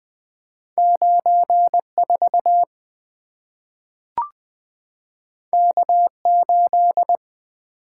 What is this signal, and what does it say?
Morse code, heard as a keyed tone, about 20 wpm, 700 Hz. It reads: 9 4 K 8
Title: Two Letter-Number Combinations - 20wpm